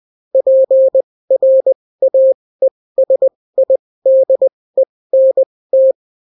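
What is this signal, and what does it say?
Morse code, heard as a keyed tone, about 20 wpm, spaced as for 16 wpm, 530 Hz. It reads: PRAESIDENT
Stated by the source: Daily Morse weather and top stories at 16 wpm, clean without QRN.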